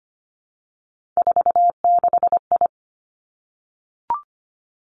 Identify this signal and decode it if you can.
Morse code, heard as a keyed tone, about 25 wpm, 700 Hz. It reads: 46I
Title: Three Challenging Letter-Number Combinations 25wpm